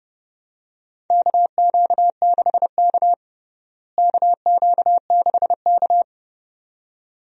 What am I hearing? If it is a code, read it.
Morse code, heard as a keyed tone, about 30 wpm, 700 Hz. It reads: KQ6K KQ6K
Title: USA Call Signs - 4 Characters - Repeated 2x - 30wpm